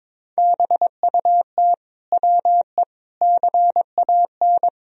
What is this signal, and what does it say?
Morse code, heard as a keyed tone, about 22 wpm, 700 Hz. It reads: BUT WE CAN
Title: Sentences from Top 100 Words and Abbreviations 22wpm